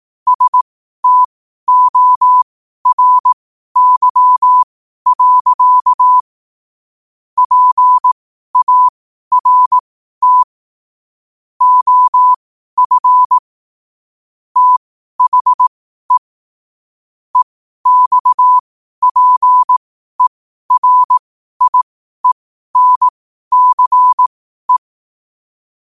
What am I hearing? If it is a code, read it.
Morse code, heard as a keyed tone, about 18 wpm, spaced as for 12 wpm, 1000 Hz. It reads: STORY. PART OF THE EXPERIENCE
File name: qotdfiles / 12wpm